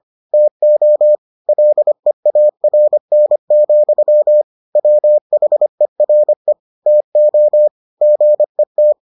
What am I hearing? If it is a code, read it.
Morse code, heard as a keyed tone, about 25 wpm, 600 Hz. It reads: TO LEARN, WHERE TO GET